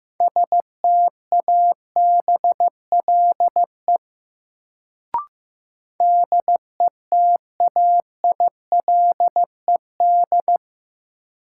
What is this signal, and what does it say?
Morse code, heard as a keyed tone, about 15 wpm, 700 Hz. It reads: STABLE DETAILED
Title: Sentences from Top 1500 Words - New Word Drill - 15wpm